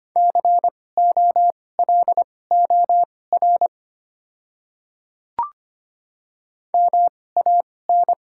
Morse code, heard as a keyed tone, about 25 wpm, spaced as for 18 wpm, 700 Hz. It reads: COLOR MAN
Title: 2x Character Spacing - Top 300 Words - 25wpm